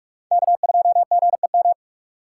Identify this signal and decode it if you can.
Morse code, heard as a keyed tone, about 60 wpm, spaced as for 53 wpm, 700 Hz. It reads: K2GEM
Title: Speed Racing - USA 1x3 Call Signs 45wpm